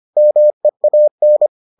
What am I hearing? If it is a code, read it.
Morse code, heard as a keyed tone, about 25 wpm, 600 Hz. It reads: MEAN